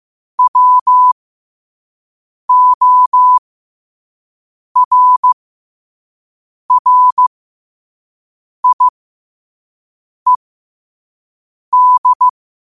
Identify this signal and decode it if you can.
Morse code, heard as a keyed tone, about 15 wpm, spaced as for 5 wpm, 1000 Hz. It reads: WORRIED
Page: 5 WPM morse code quotes for Wed, 30 Jul 2025.